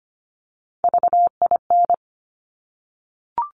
Morse code, heard as a keyed tone, about 25 wpm, 700 Hz. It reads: VIN
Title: Top Three Letter Combinations in Words 25wpm